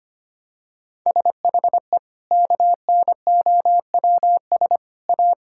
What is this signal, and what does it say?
Morse code, heard as a keyed tone, about 25 wpm, 700 Hz. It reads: SHE KNOWS A
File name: practice / 25wpm / Sentences with 4-Word Phrases - 25wpm